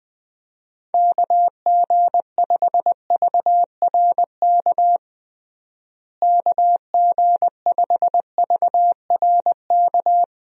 Morse code, heard as a keyed tone, about 21 wpm, 700 Hz. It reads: KG5VRK KG5VRK
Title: USA Call Signs - 6 Characters - Repeated 2x - 20wpm